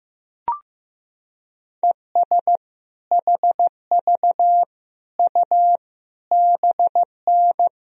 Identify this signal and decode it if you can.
Morse code, heard as a keyed tone, about 16 wpm, 700 Hz. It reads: ES HV U BN
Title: Sentences from Top 200 Words and Abbreviations 15wpm